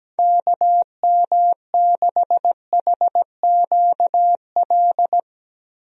Morse code, heard as a keyed tone, about 17 wpm, 700 Hz. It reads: KM6HQL
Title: USA Call Signs - 6 Characters - Repeated 3x - 17wpm